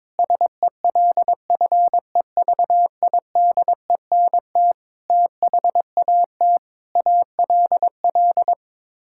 Morse code, heard as a keyed tone, about 22 wpm, 700 Hz. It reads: SELFEVIDENT THAT ALL
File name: Sets of 9 Words - 22wpm